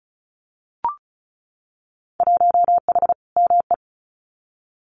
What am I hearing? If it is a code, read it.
Morse code, heard as a keyed tone, about 34 wpm, 700 Hz. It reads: 1H ME
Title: Winter Field Day Exchanges Encore 35wpm